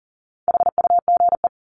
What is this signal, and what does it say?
Morse code, heard as a keyed tone, about 40 wpm, 700 Hz. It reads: HUGE
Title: Top 1001-1250 Words 40wpm